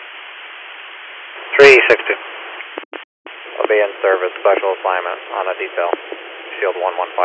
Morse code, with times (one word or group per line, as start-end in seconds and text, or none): none